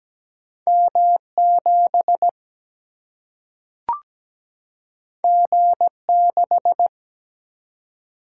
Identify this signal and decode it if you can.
Morse code, heard as a keyed tone, about 17 wpm, 700 Hz. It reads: M7 G6
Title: Single-Letter Single-Number 17wpm